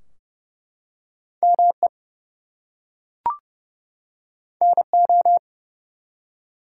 Morse code, heard as a keyed tone, about 30 wpm, 700 Hz. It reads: ME NO
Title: Two Letter Words - 30wpm